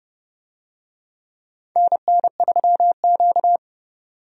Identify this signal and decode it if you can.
Morse code, heard as a keyed tone, about 32 wpm, 700 Hz. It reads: NN3Q
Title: USA Call Signs - 4 Characters - 30wpm